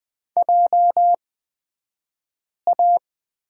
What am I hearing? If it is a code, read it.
Morse code, heard as a keyed tone, about 20 wpm, spaced as for 5 wpm, 700 Hz. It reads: JA
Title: 228 Lesson 37d - Letter x - Two Characters 20wpm